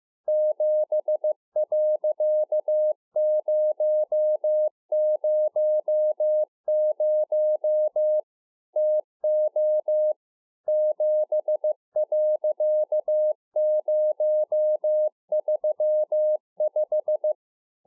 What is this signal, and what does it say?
Morse code, heard as a keyed tone, about 15 wpm, 610 Hz. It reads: 7.000 TO 7.035
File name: 2011-04-mp3 / A SMALL TRANCEIVER 2 15WPM